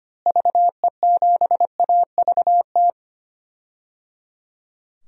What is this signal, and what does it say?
Morse code, heard as a keyed tone, about 25 wpm, 700 Hz. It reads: VE7AVT
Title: DXCC-100 Call Signs - 6 Characters - Repeated 3x - 25wpm